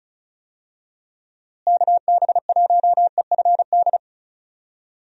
Morse code, heard as a keyed tone, about 34 wpm, 700 Hz. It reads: KB1EFD
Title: USA Call Signs - 6 Characters - Repeated 2x - 35wpm